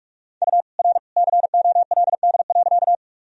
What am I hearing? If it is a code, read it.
Morse code, heard as a keyed tone, about 46 wpm, 700 Hz. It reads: U R COLD.
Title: Sentences from Top 400 Words and Abbreviations 45wpm